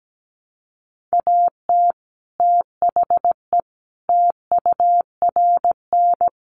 Morse code, heard as a keyed tone, about 17 wpm, 700 Hz. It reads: AT THE TURN
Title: Sets of 5 Words 17wpm